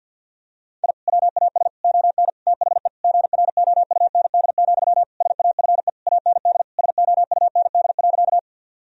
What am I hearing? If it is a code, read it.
Morse code, heard as a keyed tone, about 50 wpm, 700 Hz. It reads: I WAS ON THE GROUND, SAFE AND SOUND.
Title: Sentences from Top 1000 Words - Review Part 2 - 50wpm